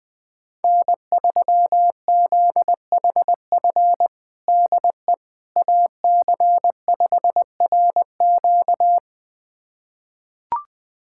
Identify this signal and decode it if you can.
Morse code, heard as a keyed tone, about 20 wpm, 700 Hz. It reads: N3ZHF DE AC5RQ
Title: USA Callsign DE Callsign - 5 Characters - 20wpm